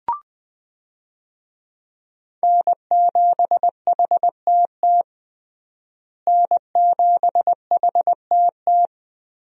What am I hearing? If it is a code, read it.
Morse code, heard as a keyed tone, about 20 wpm, 700 Hz. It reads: N7HTT N7HTT
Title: USA Call Signs - 5 Characters - Repeated 2x - 20wpm